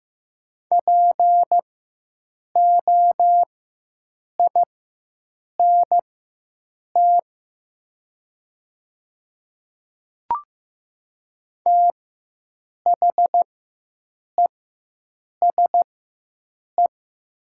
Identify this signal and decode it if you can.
Morse code, heard as a keyed tone, about 15 wpm, spaced as for 7 wpm, 700 Hz. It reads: POINT THESE
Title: 4x Character Spacing - Top 200 Words - 15wpm